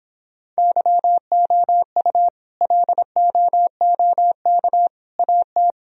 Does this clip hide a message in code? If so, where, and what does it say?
Yes: Morse code from 0.6 to 5.7 s, heard as a keyed tone, about 26 wpm, 700 Hz. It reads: YOU LOOK AT